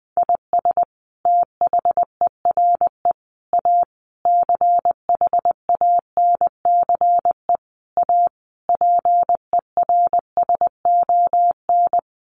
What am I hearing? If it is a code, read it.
Morse code, heard as a keyed tone, about 20 wpm, 700 Hz. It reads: IS THERE A CHANCE A PERSON
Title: Sentences from Top 900 Words 20wpm